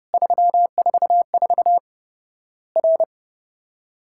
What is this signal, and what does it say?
Morse code, heard as a keyed tone, about 30 wpm, 700 Hz. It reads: 344 R
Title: Worked All Europe DX Contest - QTC Exchange - 30wpm